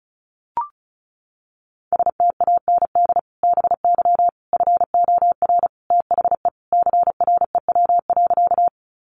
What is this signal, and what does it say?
Morse code, heard as a keyed tone, about 34 wpm, 700 Hz. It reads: STAND BY FOR THE CREW.
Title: Sentences from Top 1500 Words 35wpm